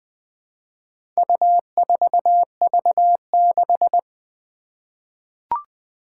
Morse code, heard as a keyed tone, about 21 wpm, 700 Hz. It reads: U4V6
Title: Four Challenging Letter-Number Combinations 20wpm